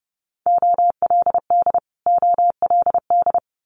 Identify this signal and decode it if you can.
Morse code, heard as a keyed tone, about 30 wpm, 700 Hz. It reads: OLD OLD
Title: Multi-letter Phonemes Repeated 3x 30wpm